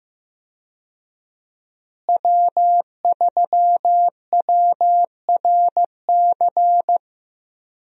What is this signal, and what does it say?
Morse code, heard as a keyed tone, about 15 wpm, 700 Hz. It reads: W3WRC
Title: USA Call Signs - 5 Characters - Repeated 2x - 15wpm